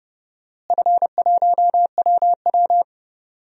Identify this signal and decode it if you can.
Morse code, heard as a keyed tone, about 30 wpm, 700 Hz. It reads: F1WW
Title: DXCC-100 Call Signs - 4 Characters - Repeated 3x - 30wpm